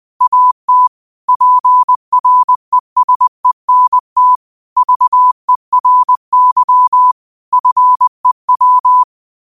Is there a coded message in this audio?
Yes: Morse code from 0.2 to 9.0 s, heard as a keyed tone, about 20 wpm, 1000 Hz. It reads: AT PRESENT VERY FEW